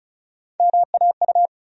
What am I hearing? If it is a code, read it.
Morse code, heard as a keyed tone, about 36 wpm, 700 Hz. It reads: MAU
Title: USA Call Sign Postfixes - 3 Letters 35wpm